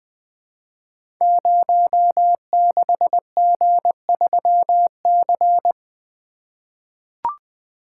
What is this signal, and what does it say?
Morse code, heard as a keyed tone, about 20 wpm, 700 Hz. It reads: 06G3C
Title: Code Groups - 5 Letter:Numbers - 20wpm